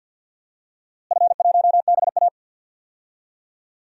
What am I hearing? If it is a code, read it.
Morse code, heard as a keyed tone, about 46 wpm, 700 Hz. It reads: F1BA
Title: DXCC-100 Call Signs - 4 Characters - Repeated 2x - 50wpm